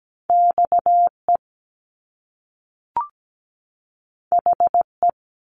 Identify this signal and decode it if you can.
Morse code, heard as a keyed tone, about 17 wpm, 700 Hz. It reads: XE HE
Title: Top Two Letter Combinations in Words 17wpm